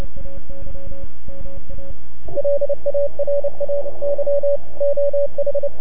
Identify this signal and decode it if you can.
Morse code, heard as a keyed tone, about 28 wpm, 570 Hz, with radio noise. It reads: LARAY OH
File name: NAQPCW-8-2009 / W1UJ NAQP CW 09 0100Z